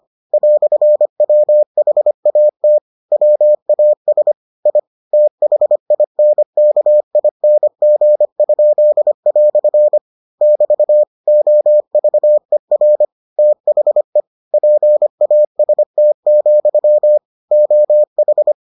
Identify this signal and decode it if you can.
Morse code, heard as a keyed tone, about 25 wpm, 600 Hz. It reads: "WHAT WAS I THINKING?" <BT> OVER THE PAST, OH